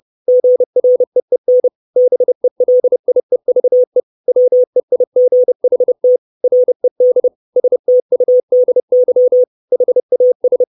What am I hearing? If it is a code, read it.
Morse code, heard as a keyed tone, about 30 wpm, 500 Hz. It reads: GREEN BELIEVE WEIGHT RED STUDY HAS
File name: 632 Words 30WPM